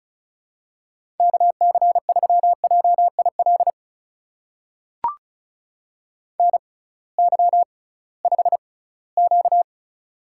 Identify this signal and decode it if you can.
Morse code, heard as a keyed tone, about 36 wpm, spaced as for 20 wpm, 700 Hz. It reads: KC3JIL NY5Q
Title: USA Call Signs - 6x Character Spacing - 35wpm